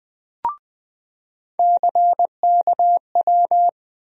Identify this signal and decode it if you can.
Morse code, heard as a keyed tone, about 20 wpm, 700 Hz. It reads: CKW
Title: USA Call Sign Postfixes - 3 Letters 20wpm